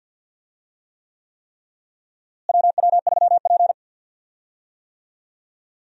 Morse code, heard as a keyed tone, about 51 wpm, 700 Hz. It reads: WW3P
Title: USA Call Signs - 4 Characters - Repeated 3x - 50wpm